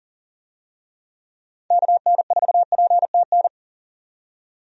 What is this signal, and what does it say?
Morse code, heard as a keyed tone, about 40 wpm, 700 Hz. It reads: KN4PTN
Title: USA Call Signs - 6 Characters - 40wpm